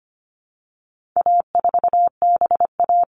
Morse code, heard as a keyed tone, about 25 wpm, 700 Hz. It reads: A4BA
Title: All Cut Numbers - 4-Digit 25wpm